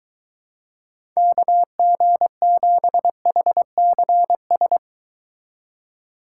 Morse code, heard as a keyed tone, about 23 wpm, 700 Hz. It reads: KG7HCS